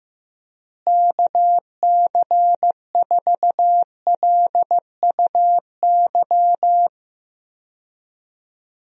Morse code, heard as a keyed tone, about 15 wpm, 700 Hz. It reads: KC4LUY
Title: USA Call Signs - 6 Characters - Repeated 3x - 15wpm